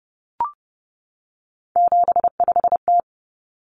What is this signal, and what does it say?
Morse code, heard as a keyed tone, about 30 wpm, 700 Hz. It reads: 75T
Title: Common Cut Numbers - 3-Digit 30wpm